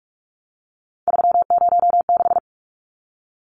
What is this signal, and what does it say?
Morse code, heard as a keyed tone, about 44 wpm, 700 Hz. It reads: 306
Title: Numbers - 3-Digit 45wpm